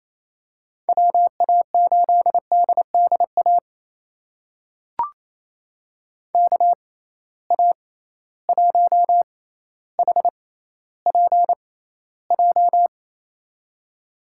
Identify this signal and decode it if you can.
Morse code, heard as a keyed tone, about 28 wpm, spaced as for 14 wpm, 700 Hz. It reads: WA8DDA KA1HPJ
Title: USA Call Signs - 6 Characters - 6x Character Spacing - 28wpm